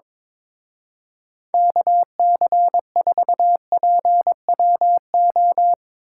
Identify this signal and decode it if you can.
Morse code, heard as a keyed tone, about 24 wpm, 700 Hz. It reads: KC4PWO
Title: Speed Racing - USA 2x3 Call Signs 22wpm